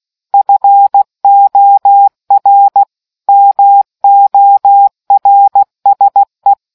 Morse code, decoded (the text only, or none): FOR MORSE